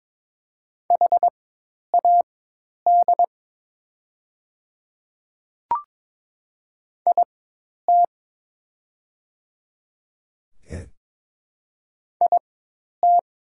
Morse code, heard as a keyed tone, about 22 wpm, spaced as for 10 wpm, 690 Hz. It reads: HAD IT IT
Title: 4x Character Spacing - Top 100 Words - 22wpm